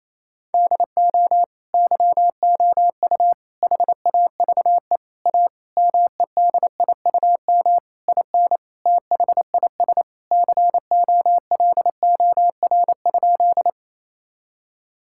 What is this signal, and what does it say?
Morse code, heard as a keyed tone, about 28 wpm, 700 Hz. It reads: DO YOU HAVE A MEDIUM IN THIS COLOR?
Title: Sentences from Top 1000 Words - Review Part 6 - 28wpm